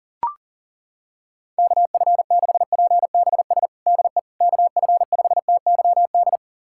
Morse code, heard as a keyed tone, about 40 wpm, 700 Hz. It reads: KF6PBS DE KF5TYD
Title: USA Callsign DE Callsign - 6 Characters - 40wpm